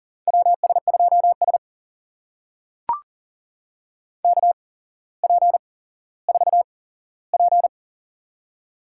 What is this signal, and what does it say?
Morse code, heard as a keyed tone, about 40 wpm, spaced as for 17 wpm, 700 Hz. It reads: WS2S KP4P
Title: USA Call Signs - 4 Characters - 8x Character Spacing - 40wpm